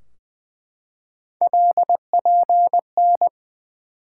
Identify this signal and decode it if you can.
Morse code, heard as a keyed tone, about 20 wpm, 700 Hz. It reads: LPN
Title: Practice QSO - Exchange 3 - More Info - 20wpm